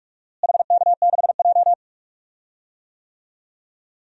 Morse code, heard as a keyed tone, about 41 wpm, 700 Hz. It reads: HK6J
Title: DXCC-100 Call Signs - 4 Characters - Repeated 3x - 45wpm